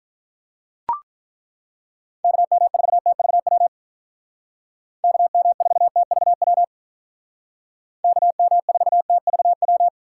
Morse code, heard as a keyed tone, about 43 wpm, 700 Hz. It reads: KM4TVW KM4TVW KM4TVW
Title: Speed Racing - USA Call Signs 35wpm